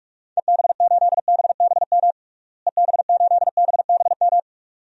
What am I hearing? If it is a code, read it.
Morse code, heard as a keyed tone, about 46 wpm, 690 Hz. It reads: EB8BBM EB8BBM
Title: DXCC-100 Call Signs - 6 Characters - Repeated 2x - 45wpm